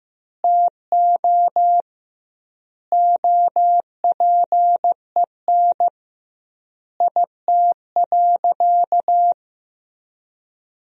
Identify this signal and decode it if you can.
Morse code, heard as a keyed tone, about 15 wpm, 700 Hz. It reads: TO OPEN IT.
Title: 2x Word-Spacing - Sentences from Top 300 Words - 15wpm